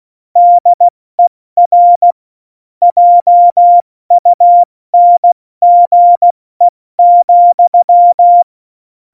Morse code, heard as a keyed tone, about 16 wpm, 700 Hz. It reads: DER JUNGE,
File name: Morsekurs / Witze / Witz 19. Februar 2019, 70 F80